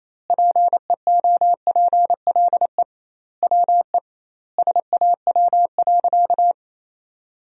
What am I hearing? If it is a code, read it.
Morse code, heard as a keyed tone, about 27 wpm, 700 Hz. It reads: PEOPLE WE SAW.
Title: 2x Word-Spacing - Sentences from Top 300 Words - 28wpm